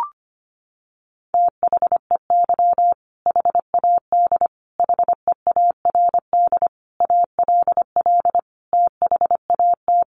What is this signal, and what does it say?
Morse code, heard as a keyed tone, about 25 wpm, 700 Hz. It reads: THEY HAD HEARD ALL THAT
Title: Sentences from Top 400 Words 25wpm